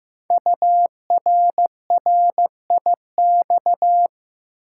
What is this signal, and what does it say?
Morse code, heard as a keyed tone, about 15 wpm, 700 Hz. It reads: URRIX
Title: Code Groups - 5 Letters - 15wpm